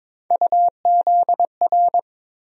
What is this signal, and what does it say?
Morse code, heard as a keyed tone, about 22 wpm, 700 Hz. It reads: UZR
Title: USA Call Sign Postfixes - 3 Letters 22wpm